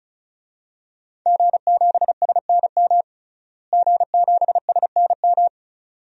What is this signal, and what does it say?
Morse code, heard as a keyed tone, about 36 wpm, 700 Hz. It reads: G7SNM G7SNM
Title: DXCC-100 Call Signs - 5 Characters - Repeated 2x - 35wpm